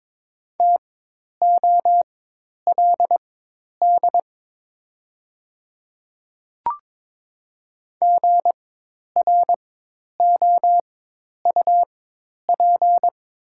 Morse code, heard as a keyed tone, about 22 wpm, spaced as for 10 wpm, 700 Hz. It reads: TOLD GROUP
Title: 4x Character Spacing - Top 500 Words - 22wpm